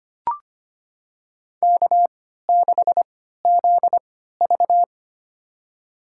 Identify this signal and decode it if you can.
Morse code, heard as a keyed tone, about 25 wpm, spaced as for 14 wpm, 700 Hz. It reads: K6ZV
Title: USA Call Signs - 4 Characters - 3x Character Spacing - 25wpm